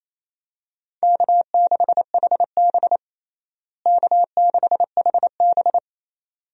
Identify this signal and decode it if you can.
Morse code, heard as a keyed tone, about 28 wpm, 700 Hz. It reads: K6HB K6HB
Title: USA Call Signs - 4 Characters - Repeated 2x - 28wpm